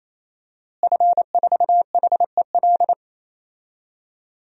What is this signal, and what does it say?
Morse code, heard as a keyed tone, about 28 wpm, 700 Hz. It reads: F4HEL
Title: DXCC-100 Call Signs - 5 Characters - Repeated 3x - 28wpm